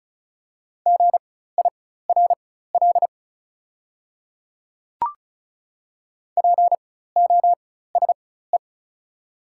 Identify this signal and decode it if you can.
Morse code, heard as a keyed tone, about 36 wpm, spaced as for 16 wpm, 700 Hz. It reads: GIRL POSE
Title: 4x Character Spacing - Top 500 Words - 35wpm